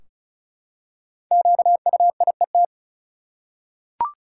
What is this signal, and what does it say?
Morse code, heard as a keyed tone, about 36 wpm, 700 Hz. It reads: QUIET
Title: Five Letter Words - 35wpm